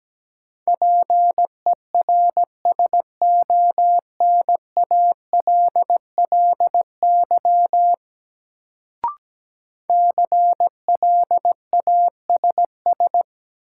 Morse code, heard as a keyed tone, about 17 wpm, 700 Hz. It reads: PERSONALLY CLASS
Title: Sentences from Top 1400 Words - New Word Drill Alt - 17wpm